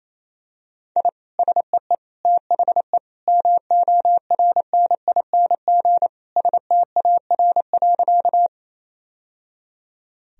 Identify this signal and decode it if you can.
Morse code, heard as a keyed tone, about 28 wpm, 700 Hz. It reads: I SEE THE MORNING STAR.
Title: Sentences from Top 1000 Words - Review Part 2 - 28wpm